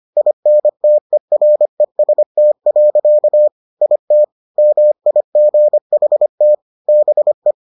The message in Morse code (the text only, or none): INTEREST. IT MIGHT BE